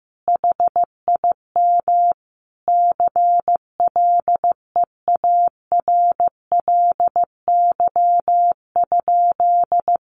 Morse code, heard as a keyed tone, about 15 wpm, 700 Hz. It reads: HIM CLEARLY?
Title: Sentences from Top 1300 Words 15wpm